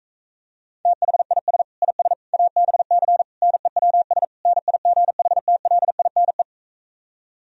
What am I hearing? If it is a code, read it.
Morse code, heard as a keyed tone, about 45 wpm, 700 Hz. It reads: THIS IS ABC NEWS NIGHTLINE